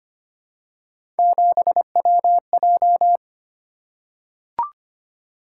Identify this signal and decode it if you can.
Morse code, heard as a keyed tone, about 25 wpm, 700 Hz. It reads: 7WJ
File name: Three Letter-Number Combinations - 25wpm